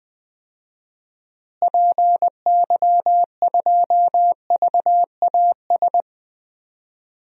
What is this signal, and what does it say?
Morse code, heard as a keyed tone, about 21 wpm, 700 Hz. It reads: PY2VAS
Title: DXCC-100 Call Signs - 6 Characters - Repeated 2x - 20wpm